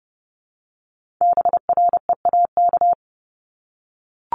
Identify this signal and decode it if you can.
Morse code, heard as a keyed tone, about 30 wpm, 700 Hz. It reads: BREAK
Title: Multi-letter Phonemes Repeated 3x 30wpm